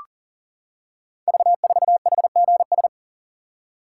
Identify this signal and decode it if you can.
Morse code, heard as a keyed tone, about 39 wpm, 690 Hz. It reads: V4HGS
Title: Code Groups - 5 Letter:Numbers - 40wpm